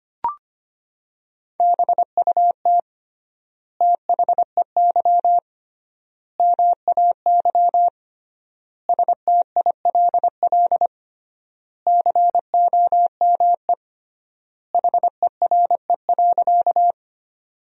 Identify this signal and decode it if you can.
Morse code, heard as a keyed tone, about 25 wpm, 700 Hz. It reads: BUT THEY MAY STILL COME HERE.
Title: 3x Word-Spacing - Sentences from Top 300 Words - 25wpm